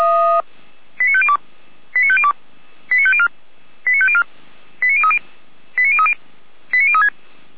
Five-tone selective call (ZVEI), heard as quick runs of five tones, tones of about 70 ms. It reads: 89502 89502 89504 89504 89030 89030 89036